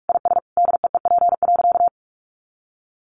Morse code, heard as a keyed tone, about 44 wpm, 700 Hz. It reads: IS DEEP.
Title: Sentences from Top 500 Words 45wpm